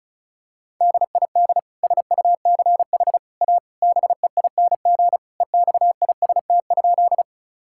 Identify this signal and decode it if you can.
Morse code, heard as a keyed tone, about 34 wpm, 700 Hz. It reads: DID SUCH A BEING EXIST?
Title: Sentences from Top 1000 Words - Review Part 5 - 35wpm